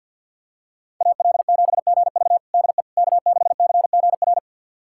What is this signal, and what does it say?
Morse code, heard as a keyed tone, about 50 wpm, 700 Hz. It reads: AL7GV DE K6CGR
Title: USA Callsign DE Callsign - 5 Characters - 50wpm